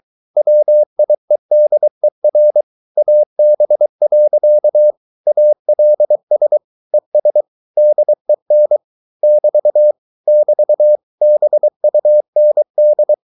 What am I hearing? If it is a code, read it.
Morse code, heard as a keyed tone, about 23 wpm, 600 Hz. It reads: WIEDER AB. ALS ES DEN <BT> <BT> BUND